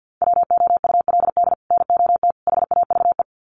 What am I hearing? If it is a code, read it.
Morse code, heard as a keyed tone, about 50 wpm, 700 Hz. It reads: WOULD NOT HAVE
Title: Sets of 3 Words 50wpm